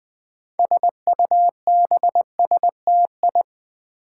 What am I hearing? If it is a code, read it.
Morse code, heard as a keyed tone, about 20 wpm, 700 Hz. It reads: SUBSTI
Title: Build A Word - 6 Letters At A Time - 12-Letter Words - 20wpm